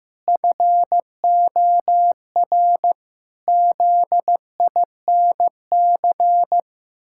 Morse code, heard as a keyed tone, about 15 wpm, 700 Hz. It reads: FOR ZINC
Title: Pangram Challenge v2 - 15wpm